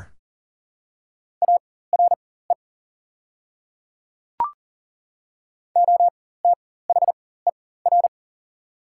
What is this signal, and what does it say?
Morse code, heard as a keyed tone, about 40 wpm, spaced as for 19 wpm, 690 Hz. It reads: ARE OTHER
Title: 4x Character Spacing - Top 300 Words - 40wpm